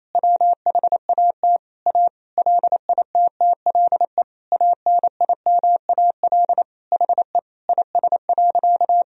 Morse code, heard as a keyed tone, about 28 wpm, 700 Hz. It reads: WHAT A LITTLE ANIMAL HE IS.
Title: Sentences from Top 1000 Words - Review Part 1 - 28wpm